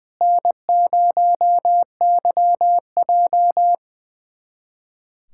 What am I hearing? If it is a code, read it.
Morse code, heard as a keyed tone, about 21 wpm, 700 Hz. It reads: N0YJ
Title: USA Callsign DE Callsign - 4 Characters - 20wpm